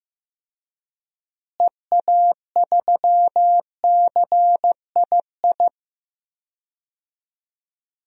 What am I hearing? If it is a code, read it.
Morse code, heard as a keyed tone, about 15 wpm, 700 Hz. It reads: EA3CII
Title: DXCC-100 Call Signs - 6 Characters - Repeated 2x - 15wpm